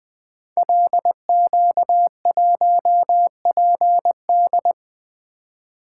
Morse code, heard as a keyed tone, about 20 wpm, 700 Hz. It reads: LQ1PD
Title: Code Groups - 5 Letter:Numbers - 20wpm